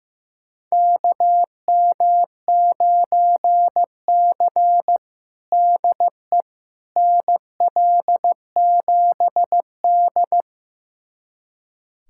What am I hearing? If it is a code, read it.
Morse code, heard as a keyed tone, about 15 wpm, 700 Hz. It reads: KM9C DE NL7D
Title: USA Callsign DE Callsign - 4 Characters - 15wpm